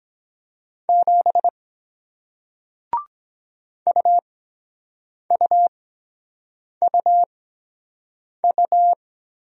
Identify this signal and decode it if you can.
Morse code, heard as a keyed tone, about 23 wpm, spaced as for 7 wpm, 700 Hz. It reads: 7 UUUU